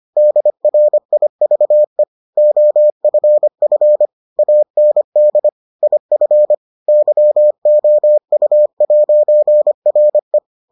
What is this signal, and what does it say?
Morse code, heard as a keyed tone, about 25 wpm, 600 Hz. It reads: DRIVE OFF AND IF YOU'RE